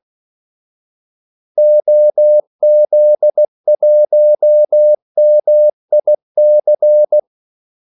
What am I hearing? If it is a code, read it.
Morse code, heard as a keyed tone, about 16 wpm, 600 Hz. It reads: OZ1MIC
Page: Kaldesignaler 16 wpm